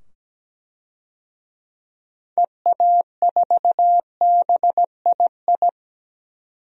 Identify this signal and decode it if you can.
Morse code, heard as a keyed tone, about 17 wpm, 700 Hz. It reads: EA4BII
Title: DXCC-100 Call Signs - 6 Characters - 17wpm